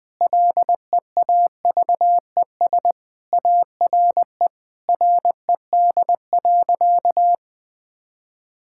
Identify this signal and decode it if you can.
Morse code, heard as a keyed tone, about 21 wpm, 700 Hz. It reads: LEAVES ARE RED.
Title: Easy Sentences with Hints - 20wpm